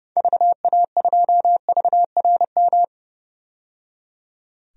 Morse code, heard as a keyed tone, about 30 wpm, 700 Hz. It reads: VA2VRM
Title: DXCC-100 Call Signs - 6 Characters - 30wpm